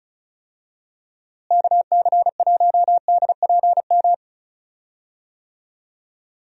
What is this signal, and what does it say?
Morse code, heard as a keyed tone, about 36 wpm, 700 Hz. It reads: KC1DPM
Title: USA Call Signs - 6 Characters - Repeated 3x - 35wpm